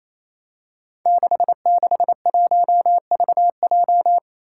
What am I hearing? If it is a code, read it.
Morse code, heard as a keyed tone, about 27 wpm, 700 Hz. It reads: 661VJ
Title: Five Challenging Letter-Number Combinations 28wpm